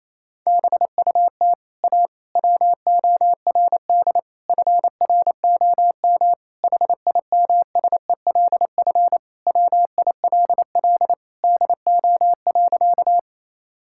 Morse code, heard as a keyed tone, about 28 wpm, 700 Hz. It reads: BUT A WORD FROM HIMSELF WILL DO.
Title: Sentences from Top 1600 Words 28wpm